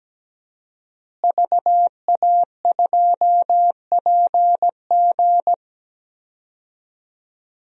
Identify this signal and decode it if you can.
Morse code, heard as a keyed tone, about 17 wpm, 700 Hz. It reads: VA2PG
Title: DXCC-100 Call Signs - 5 Characters - Repeated 3x - 17wpm